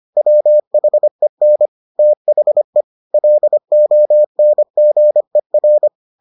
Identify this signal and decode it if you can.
Morse code, heard as a keyed tone, about 25 wpm, 600 Hz. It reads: WHEN THE LONGER